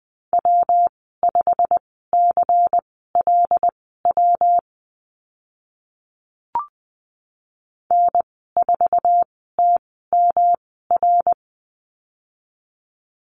Morse code, heard as a keyed tone, about 20 wpm, spaced as for 15 wpm, 700 Hz. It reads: W5CLW N4TMR
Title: 140 Lesson 23g - Letter f - Call Signs 20wpm